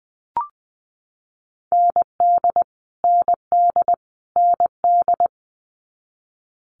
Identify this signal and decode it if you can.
Morse code, heard as a keyed tone, about 19 wpm, 700 Hz. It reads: ND ND ND
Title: Multi-letter Phonemes Repeated 3x 20wpm